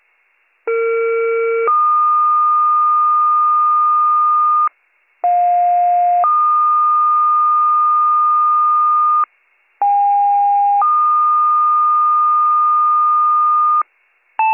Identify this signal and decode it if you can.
Two-tone page, heard as a one-second tone, then a three-second tone, about 460 then 1220 Hz, about 710 then 1220 Hz, about 810 then 1220 Hz. It is heard from 0.7 to 13.8 s.